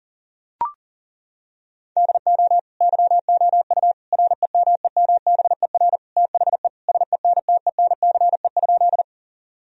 Morse code, heard as a keyed tone, about 40 wpm, 700 Hz. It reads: DO YOU REMEMBER THE SENTENCE?
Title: Sentences from Top 1000 Words - Review Part 1 - 40wpm